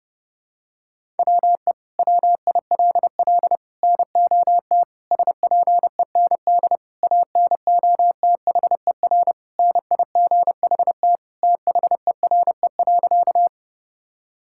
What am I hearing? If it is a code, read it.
Morse code, heard as a keyed tone, about 30 wpm, 700 Hz. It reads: WE WILL NOT SPEND ANOTHER NIGHT THERE.
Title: Sentences from Top 1000 Words - Review Part 5 - 30wpm